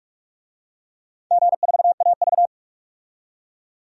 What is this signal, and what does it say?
Morse code, heard as a keyed tone, about 46 wpm, 690 Hz. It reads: G4AV
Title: DXCC-100 Call Signs - 4 Characters - 45wpm